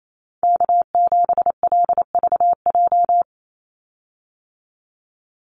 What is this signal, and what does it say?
Morse code, heard as a keyed tone, about 28 wpm, 700 Hz. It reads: K7LVJ
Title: Call Signs Repeated 3x - 28wpm